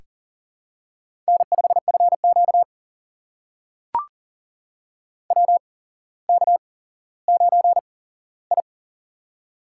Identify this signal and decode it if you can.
Morse code, heard as a keyed tone, about 38 wpm, spaced as for 17 wpm, 700 Hz. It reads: N5FQ WK9I
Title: USA Call Signs - 4 Characters - 8x Character Spacing - 40wpm